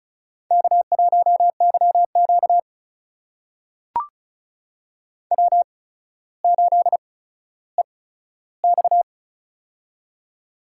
Morse code, heard as a keyed tone, about 36 wpm, spaced as for 15 wpm, 700 Hz. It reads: K1YQ W8EX
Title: USA Call Signs - 4 Characters - 8x Character Spacing - 35wpm